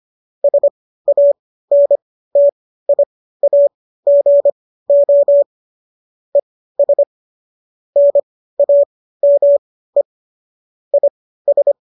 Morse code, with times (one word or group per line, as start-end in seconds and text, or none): 0.4-5.4 SANTIAGO
6.3-7.0 ES
8.0-10.0 NAME
10.9-11.7 IS